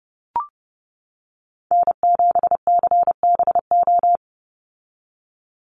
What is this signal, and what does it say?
Morse code, heard as a keyed tone, about 30 wpm, 700 Hz. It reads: N7CBO
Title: USA Call Signs 30wpm